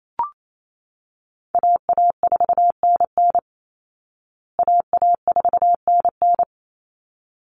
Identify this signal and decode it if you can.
Morse code, heard as a keyed tone, about 28 wpm, 700 Hz. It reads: AA4NN AA4NN
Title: Call Signs Repeated 3x - 28wpm